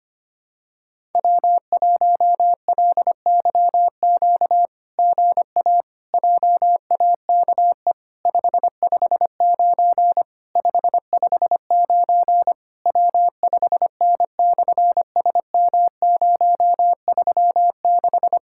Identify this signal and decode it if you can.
Morse code, heard as a keyed tone, about 24 wpm, 700 Hz. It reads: W1LYQ GA JAKE 559 559 W5N/SM036
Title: SOTA QSO Practice 25wpm